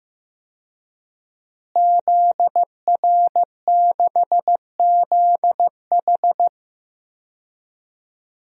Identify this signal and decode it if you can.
Morse code, heard as a keyed tone, about 15 wpm, 700 Hz. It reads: ZR6ZH
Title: DXCC-100 Call Signs - 5 Characters - Repeated 3x - 15wpm